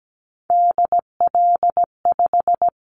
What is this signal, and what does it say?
Morse code, heard as a keyed tone, about 17 wpm, 700 Hz. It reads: DL5
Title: Callsign Prefixes for Top 50 DXCC - 17wpm